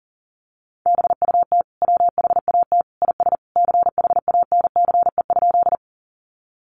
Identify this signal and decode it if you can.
Morse code, heard as a keyed tone, about 37 wpm, 700 Hz. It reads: BUT WHAT IS CHANCE?
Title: Sentences from Top 800 Words 40wpm